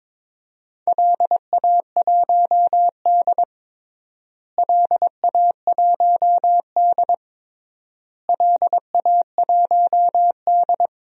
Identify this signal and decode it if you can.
Morse code, heard as a keyed tone, about 22 wpm, 700 Hz. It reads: LA1D LA1D LA1D
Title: DXCC-100 Call Signs - 4 Characters - Repeated 3x - 22wpm